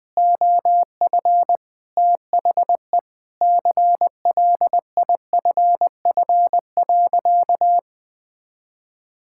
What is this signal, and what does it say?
Morse code, heard as a keyed tone, about 20 wpm, 700 Hz. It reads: OF THE CLIFF.
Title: Sentences with 5-Word Phrases - Spoken Beginning - 20wpm